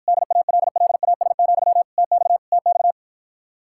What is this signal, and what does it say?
Morse code, heard as a keyed tone, about 53 wpm, 700 Hz. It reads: DALLAS, TX TX